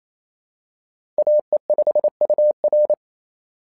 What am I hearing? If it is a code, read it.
Morse code, heard as a keyed tone, about 28 wpm, 600 Hz. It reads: AE5UR
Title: SOTA QSO Practice 28wpm